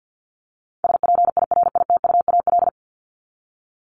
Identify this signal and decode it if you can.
Morse code, heard as a keyed tone, about 50 wpm, 700 Hz. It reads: SPIRITUAL
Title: Top 1501-1750 Words 50wpm